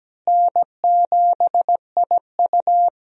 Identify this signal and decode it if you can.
Morse code, heard as a keyed tone, about 17 wpm, 700 Hz. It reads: N7IU
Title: USA Call Signs - 4 Characters - Repeated 3x - 17wpm